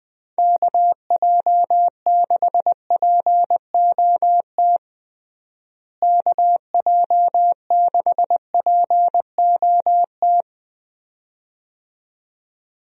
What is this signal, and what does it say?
Morse code, heard as a keyed tone, about 20 wpm, 700 Hz. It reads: KJ6POT KJ6POT
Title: USA Call Signs - 6 Characters - Repeated 2x - 20wpm